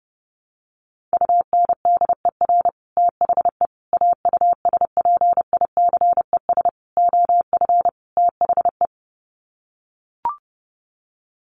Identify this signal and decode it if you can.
Morse code, heard as a keyed tone, about 30 wpm, 700 Hz. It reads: UNDER THE AUSPICES OF THE
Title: Sets of 5 Words Encore2 30wpm